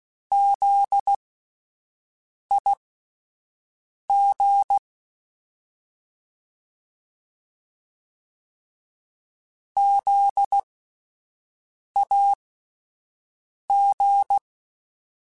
Morse code, heard as a keyed tone, about 16 wpm, spaced as for 5 wpm, 800 Hz. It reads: ZIG ZAG
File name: Spaced Alphas / Spaced Alphas mp3s / Zs sp @10